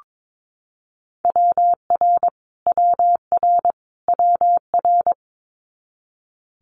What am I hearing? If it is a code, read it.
Morse code, heard as a keyed tone, about 22 wpm, 700 Hz. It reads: WR WR WR
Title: Multi-letter Phonemes Repeated 3x 22wpm